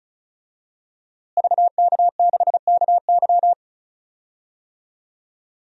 Morse code, heard as a keyed tone, about 34 wpm, 700 Hz. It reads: VK6KY
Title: DXCC-100 Call Signs - 5 Characters - Repeated 2x - 35wpm